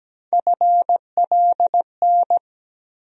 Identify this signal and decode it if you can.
Morse code, heard as a keyed tone, about 17 wpm, 700 Hz. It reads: FLN
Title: USA Call Sign Postfixes - 3 Letters 17wpm